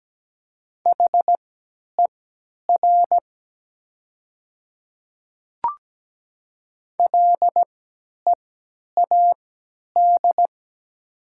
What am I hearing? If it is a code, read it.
Morse code, heard as a keyed tone, about 17 wpm, spaced as for 10 wpm, 700 Hz. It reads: HER LEAD
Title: 3x Character Spacing - Top 500 Words - 17wpm